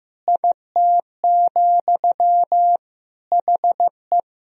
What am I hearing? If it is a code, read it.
Morse code, heard as a keyed tone, about 16 wpm, 700 Hz. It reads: IT, HE
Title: Sentences from Top 1000 Words - Review Part 4 - 15wpm